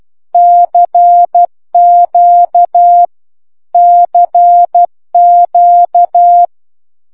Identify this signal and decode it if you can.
Morse code, heard as a keyed tone, about 12 wpm, 700 Hz. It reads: CQ CQ
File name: cw c q